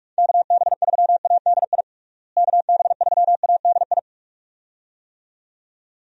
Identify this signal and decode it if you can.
Morse code, heard as a keyed tone, about 46 wpm, 700 Hz. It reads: KB3ADI KB3ADI
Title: USA Call Signs - 6 Characters - Repeated 2x - 45wpm